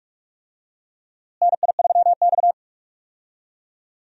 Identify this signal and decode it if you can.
Morse code, heard as a keyed tone, about 42 wpm, 700 Hz. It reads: NI3X
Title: USA Call Signs - 4 Characters - Repeated 2x - 45wpm